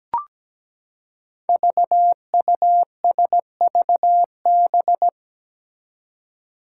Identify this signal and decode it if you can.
Morse code, heard as a keyed tone, about 17 wpm, 700 Hz. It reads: VUSVB
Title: Five Challenging Letter-Number Combinations 17wpm